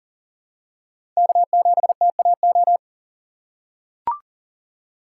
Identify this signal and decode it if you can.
Morse code, heard as a keyed tone, about 40 wpm, 700 Hz. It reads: K7TAO
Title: USA Call Signs - 5 Characters - 6x Character Spacing - 40wpm